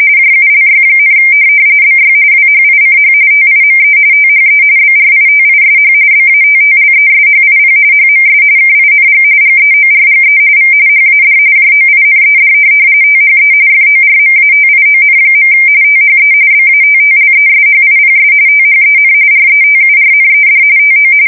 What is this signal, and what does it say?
RTTY, about 45.45 baud. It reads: JULY, BY WHICH TIME THE BRITISH BATTLESHIP HMS KING GEORGE V HAD JOINED THE FORMATION. THE BATTLESHIPS THEN RETURNED TO
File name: fsk170 / USS-Missouri